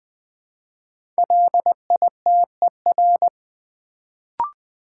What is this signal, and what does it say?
Morse code, heard as a keyed tone, about 20 wpm, 700 Hz. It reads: LITER
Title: Build A Word - 1 Letter At A Time - 5-Letter Words - 20wpm